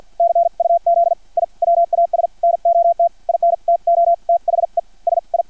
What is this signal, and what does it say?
Morse code, heard as a keyed tone, about 50 wpm, spaced as for 43 wpm, 680 Hz. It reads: CUZ I WAS NOT INTO THE SI
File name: archived / CRSnet